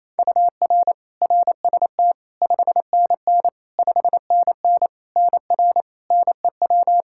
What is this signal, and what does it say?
Morse code, heard as a keyed tone, about 28 wpm, 700 Hz. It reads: UR RST 5NN 5NN NR NEW
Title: K1USN SST QSO Practice 28wpm